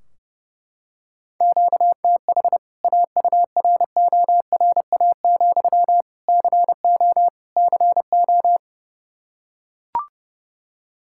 Morse code, heard as a keyed tone, about 32 wpm, 700 Hz. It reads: QTH AURORA, CO CO
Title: K1USN SST QSO Practice 30wpm